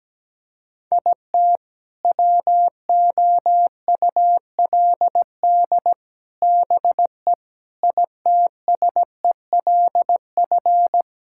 Morse code, heard as a keyed tone, about 17 wpm, 700 Hz. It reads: IT WOULD BE ITSELF
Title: Sentences from Top 1000 Words - Review Part 5 - 17wpm